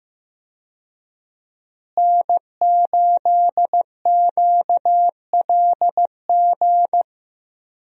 Morse code, heard as a keyed tone, about 15 wpm, 700 Hz. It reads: N8QLG
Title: USA Call Signs - 5 Characters - Repeated 2x - 15wpm